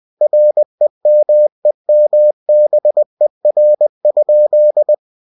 Morse code, heard as a keyed tone, about 20 wpm, 600 Hz. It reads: REMEMBER?
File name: cw-books / winnie-the-pooh / winnie-20wpm-page-0011